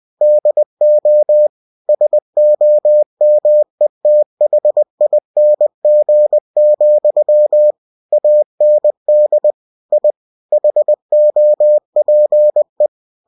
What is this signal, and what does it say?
Morse code, heard as a keyed tone, about 20 wpm, 600 Hz. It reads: DO SOMETHING, AND I HOPE
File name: cw-books / winnie-the-pooh / winnie-20wpm-page-0093